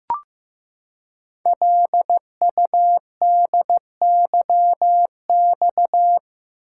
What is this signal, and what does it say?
Morse code, heard as a keyed tone, about 15 wpm, 700 Hz. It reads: LUDYX
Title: Code Groups - 5 Letters - 15wpm